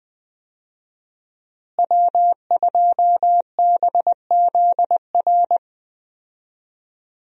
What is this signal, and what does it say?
Morse code, heard as a keyed tone, about 20 wpm, 700 Hz. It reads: W2BZR
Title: USA Call Signs - 5 Characters - Repeated 3x - 20wpm